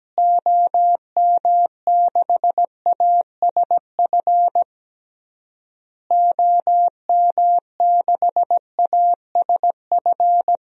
Morse code, heard as a keyed tone, about 17 wpm, 700 Hz. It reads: OM6ASF OM6ASF
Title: DXCC-100 Call Signs - 6 Characters - Repeated 3x - 17wpm